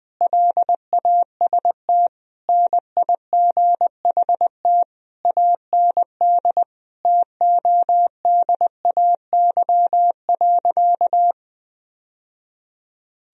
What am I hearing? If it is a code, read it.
Morse code, heard as a keyed tone, about 21 wpm, 700 Hz. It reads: LAST NIGHT AND TODAY.
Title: Sentences from Top 1000 Words - Review Part 4 - 20wpm